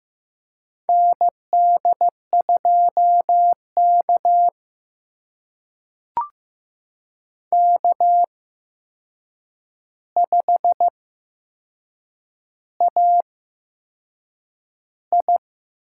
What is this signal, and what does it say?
Morse code, heard as a keyed tone, about 15 wpm, spaced as for 6 wpm, 700 Hz. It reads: ND2K K5AI
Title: USA Call Signs - 4 Characters - 8x Character Spacing - 15wpm